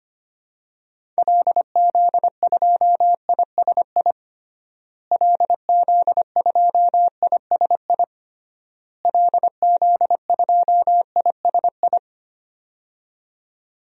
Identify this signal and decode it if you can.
Morse code, heard as a keyed tone, about 25 wpm, 700 Hz. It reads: LZ2ISI LZ2ISI LZ2ISI
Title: DXCC-100 Call Signs - 6 Characters - Repeated 3x - 25wpm